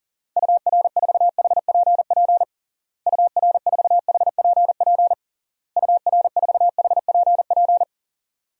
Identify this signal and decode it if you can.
Morse code, heard as a keyed tone, about 40 wpm, 700 Hz. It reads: UR4HPP UR4HPP UR4HPP
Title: DXCC-100 Call Signs - 6 Characters - Repeated 3x - 40wpm